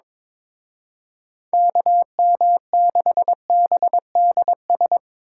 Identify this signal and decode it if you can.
Morse code, heard as a keyed tone, about 25 wpm, 700 Hz. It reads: KM6BDS
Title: Speed Racing - USA 2x3 Call Signs 22wpm